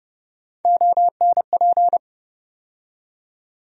Morse code, heard as a keyed tone, about 30 wpm, 700 Hz. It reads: ONP
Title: USA Call Sign Postfixes - 3 Letters 30wpm